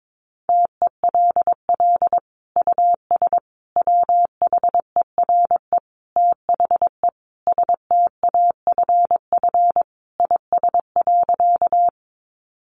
Morse code, heard as a keyed tone, about 22 wpm, 700 Hz. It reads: TELL US WHERE THE STAFF IS.
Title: Sentences from Top 1200 Words 22wpm